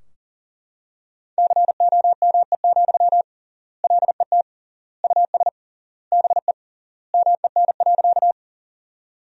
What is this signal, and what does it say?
Morse code, heard as a keyed tone, about 40 wpm, 700 Hz. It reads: COME, LET US BE MEN.
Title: 3x Word-Spacing - Sentences from Top 300 Words - 40wpm